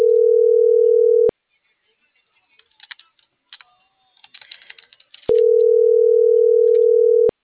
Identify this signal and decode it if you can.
Ringback tone (North America), 440 + 480 Hz.